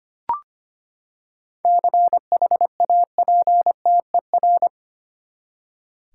Morse code, heard as a keyed tone, about 25 wpm, 700 Hz. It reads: CHAPTER
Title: Seven Letter Words - 25wpm